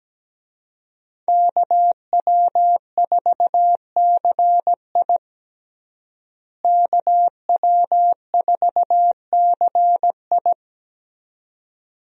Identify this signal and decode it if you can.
Morse code, heard as a keyed tone, about 17 wpm, 700 Hz. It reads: KW4CI KW4CI
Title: USA Call Signs - 5 Characters - Repeated 2x - 17wpm